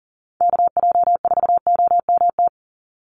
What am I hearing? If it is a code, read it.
Morse code, heard as a keyed tone, about 40 wpm, 700 Hz. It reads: KJ4OMT
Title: USA Call Signs 40wpm